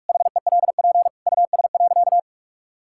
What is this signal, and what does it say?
Morse code, heard as a keyed tone, about 46 wpm, 700 Hz. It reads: HELP US.
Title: Sentences from Top 1000 Words - Review Part 5 - 45wpm